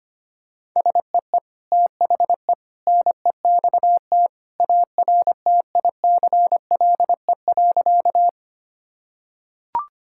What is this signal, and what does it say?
Morse code, heard as a keyed tone, about 25 wpm, 700 Hz. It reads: SEE THE NEXT ARTICLE.
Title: Sentences from Top 1000 Words - Review Part 5 - 25wpm